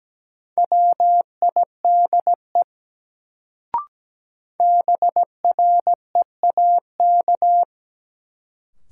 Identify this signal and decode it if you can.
Morse code, heard as a keyed tone, about 17 wpm, 700 Hz. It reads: WIDE BREAK
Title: Sentences from Top 900 Words - New Word Drill - 17wpm